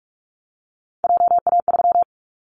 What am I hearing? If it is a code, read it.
Morse code, heard as a keyed tone, about 44 wpm, 700 Hz. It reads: JA3
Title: Callsign Prefixes for Top 25 DXCC - Repeated 3x - 45wpm